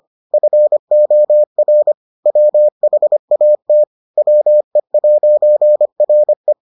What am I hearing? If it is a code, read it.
Morse code, heard as a keyed tone, about 25 wpm, 600 Hz. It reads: FOR WHAT WE'RE